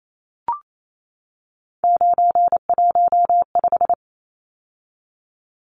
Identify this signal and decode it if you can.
Morse code, heard as a keyed tone, about 28 wpm, 700 Hz. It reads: 915
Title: Numbers - 3-Digit 28wpm